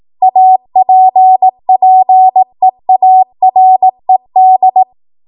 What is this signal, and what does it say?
Morse code, heard as a keyed tone, about 18 wpm, 750 Hz. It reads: APPEARED
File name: Archive / 18 WPM / 18WPM